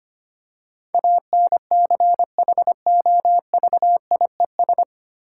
Morse code, heard as a keyed tone, about 25 wpm, 700 Hz. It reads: ANCHOVIES
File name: practice / 25wpm / Difficult Words with Hints - 25wpm